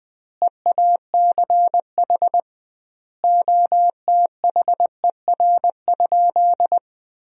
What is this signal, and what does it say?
Morse code, heard as a keyed tone, about 20 wpm, 700 Hz. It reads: EACH OTHER?
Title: 2x Word-Spacing - Sentences from Top 200 Words - 20wpm